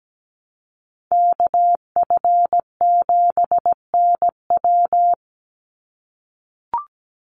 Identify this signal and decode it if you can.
Morse code, heard as a keyed tone, about 17 wpm, 700 Hz. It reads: KF7NW
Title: USA Call Signs 17wpm